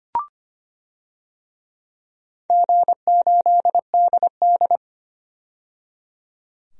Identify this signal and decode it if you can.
Morse code, heard as a keyed tone, about 25 wpm, 700 Hz. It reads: G8DD
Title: DXCC-100 Call Signs - 4 Characters - 25wpm